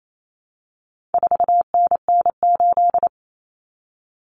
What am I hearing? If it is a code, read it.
Morse code, heard as a keyed tone, about 28 wpm, 700 Hz. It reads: 4NN8
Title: Common Cut Numbers - 4-Digit 28wpm